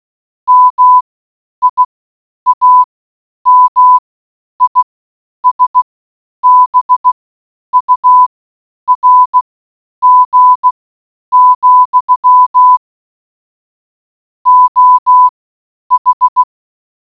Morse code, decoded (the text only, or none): MIAMISBURG, OH